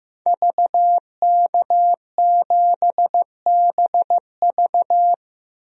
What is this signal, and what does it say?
Morse code, heard as a keyed tone, about 16 wpm, 700 Hz. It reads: VK7BV
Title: DXCC-100 Call Signs - 5 Characters - Repeated 2x - 15wpm